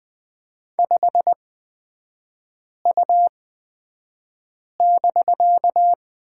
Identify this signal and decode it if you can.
Morse code, heard as a keyed tone, about 20 wpm, spaced as for 5 wpm, 700 Hz. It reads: 5U<BK>
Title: 250 Lesson 40e - BK - Three Characters 20wpm